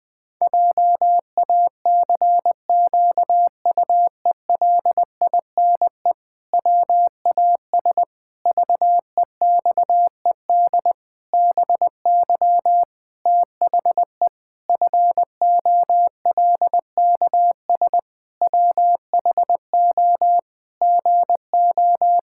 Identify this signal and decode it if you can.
Morse code, heard as a keyed tone, about 20 wpm, 700 Hz. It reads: JACQUELINE WAS VEXED BY THE FOLKS WHO GO
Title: Pangram Challenge v2 - 20wpm